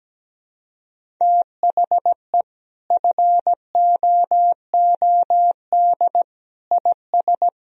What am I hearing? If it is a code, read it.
Morse code, heard as a keyed tone, about 17 wpm, 700 Hz. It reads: THE FOOD IS
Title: Sentences from Top 1000 Words - Review Part 1 - 17wpm